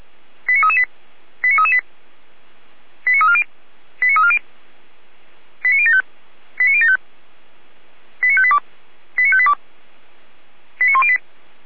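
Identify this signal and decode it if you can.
Five-tone selective call (ZVEI), heard as quick runs of five tones, tones of about 80 ms. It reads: 89338 89338 89340 89340 89075 89075 89582 89582 89108
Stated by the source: ZVEI five-tone selective call received by SDR.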